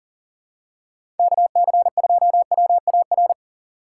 Morse code, heard as a keyed tone, about 40 wpm, 700 Hz. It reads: KC2WAR
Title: USA Call Signs - 6 Characters - 3x Character Spacing - 40wpm